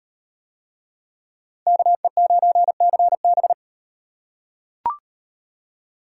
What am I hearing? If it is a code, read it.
Morse code, heard as a keyed tone, about 38 wpm, 700 Hz. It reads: KE9CB